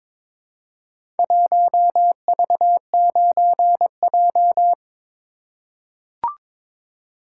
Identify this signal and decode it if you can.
Morse code, heard as a keyed tone, about 23 wpm, 700 Hz. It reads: 1V9J
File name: practice / 22wpm / Four Letter-Number Combinations - 22wpm